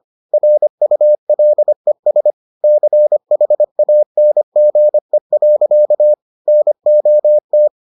Morse code, heard as a keyed tone, about 25 wpm, 600 Hz. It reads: RULES CHANGE. NOT